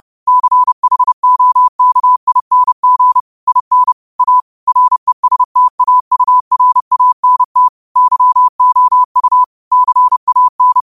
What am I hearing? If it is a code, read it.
Morse code, heard as a keyed tone, about 30 wpm, 1000 Hz. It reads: CHOKING IN A RESTAURANT YOU CAN
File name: qotdfiles / 30wpm